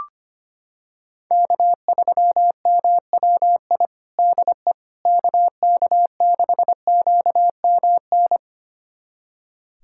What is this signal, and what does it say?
Morse code, heard as a keyed tone, about 25 wpm, 700 Hz. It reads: K3MWI DE KK6QMN
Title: USA Callsign DE Callsign 25wpm